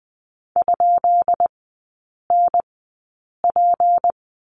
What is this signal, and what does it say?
Morse code, heard as a keyed tone, about 20 wpm, spaced as for 8 wpm, 700 Hz. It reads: ?NP
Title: 152 Lesson 25e - Letter p - Three Characters 20wpm